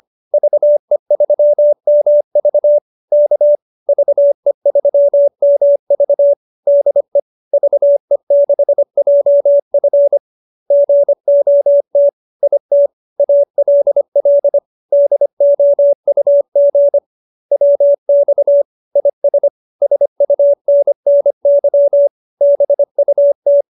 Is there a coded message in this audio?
Yes: Morse code from 0.3 to 23.6 s, heard as a keyed tone, about 25 wpm, 570 Hz. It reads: VE3MV K VE3MV DE VE6JF GOT IT ALL DOUG WX IS SUNNY BUT